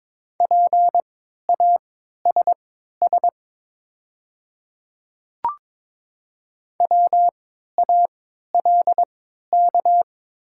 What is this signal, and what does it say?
Morse code, heard as a keyed tone, about 22 wpm, spaced as for 13 wpm, 690 Hz. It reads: PASS WALK
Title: 3x Character Spacing - Top 500 Words - 22wpm